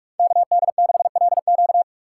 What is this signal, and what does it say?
Morse code, heard as a keyed tone, about 44 wpm, 700 Hz. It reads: KD6LQ